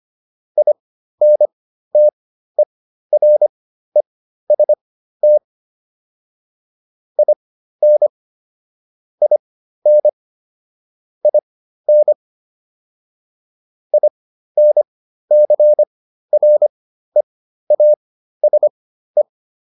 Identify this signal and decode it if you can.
Morse code, heard as a keyed tone, about 25 wpm, spaced as for 13 wpm, 600 Hz. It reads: INTEREST IN IN IN INCREASE